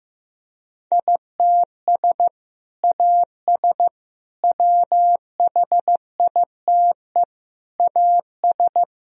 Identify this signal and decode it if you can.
Morse code, heard as a keyed tone, about 16 wpm, 700 Hz. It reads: ITS AS WHITE AS
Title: Sentences from Top 1000 Words - Review Part 3 - 15wpm